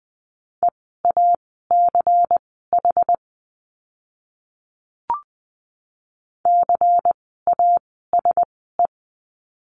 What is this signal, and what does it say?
Morse code, heard as a keyed tone, about 20 wpm, spaced as for 14 wpm, 700 Hz. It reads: EACH CASE
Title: 094 Lesson 16f - Letter c - Words 20wpm